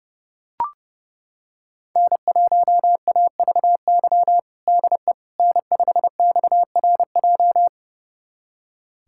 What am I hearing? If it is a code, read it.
Morse code, heard as a keyed tone, about 32 wpm, 700 Hz. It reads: N1AVY DE N5XRJ
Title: USA Callsign DE Callsign - 5 Characters - 30wpm